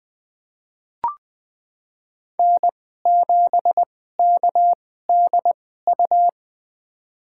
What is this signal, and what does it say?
Morse code, heard as a keyed tone, about 21 wpm, spaced as for 15 wpm, 700 Hz. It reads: N7KDU
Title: USA Call Signs - 5 Characters - 2x Character Spacing - 20wpm